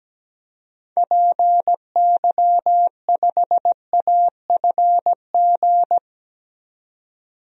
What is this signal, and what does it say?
Morse code, heard as a keyed tone, about 17 wpm, 700 Hz. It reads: PY5AFG
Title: DXCC-100 Call Signs - 6 Characters - 17wpm